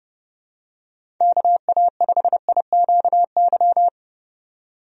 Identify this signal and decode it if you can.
Morse code, heard as a keyed tone, about 29 wpm, 700 Hz. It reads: KA5IQY
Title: USA Call Signs - 6 Characters - 30wpm